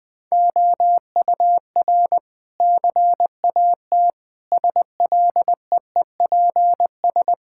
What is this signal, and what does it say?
Morse code, heard as a keyed tone, about 20 wpm, 700 Hz. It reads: OUR CAT SLEEPS
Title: Sentences with 5-Word Phrases - Spoken Beginning - 20wpm